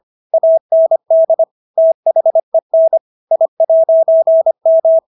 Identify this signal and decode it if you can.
Morse code, heard as a keyed tone, about 25 wpm, 650 Hz. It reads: AND THEN I'M